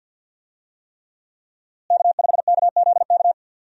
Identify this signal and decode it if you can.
Morse code, heard as a keyed tone, about 53 wpm, 700 Hz. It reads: K5KZK
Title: USA Call Signs - 5 Characters - Repeated 3x - 50wpm